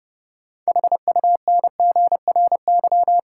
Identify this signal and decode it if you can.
Morse code, heard as a keyed tone, about 30 wpm, 700 Hz. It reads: HUNGRY
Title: Sentences from Top 1400 Words - New Word Drill Alt - 30wpm